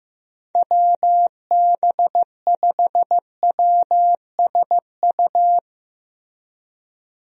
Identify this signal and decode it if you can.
Morse code, heard as a keyed tone, about 15 wpm, 700 Hz. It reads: WB5WSU
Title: USA Call Signs - 6 Characters - Repeated 2x - 15wpm